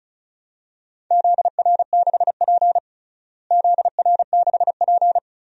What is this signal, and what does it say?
Morse code, heard as a keyed tone, about 36 wpm, 700 Hz. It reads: ZR6P ZR6P
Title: DXCC-100 Call Signs - 4 Characters - Repeated 3x - 35wpm